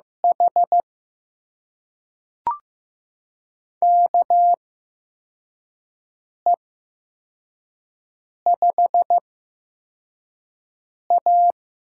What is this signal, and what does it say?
Morse code, heard as a keyed tone, about 15 wpm, spaced as for 4 wpm, 700 Hz. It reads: H KE5A
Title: USA Call Signs - 8x Character Spacing - 15wpm